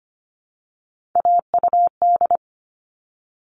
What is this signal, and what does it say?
Morse code, heard as a keyed tone, about 25 wpm, 700 Hz. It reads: AUD
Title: All Cut Numbers - 3-Digit 25wpm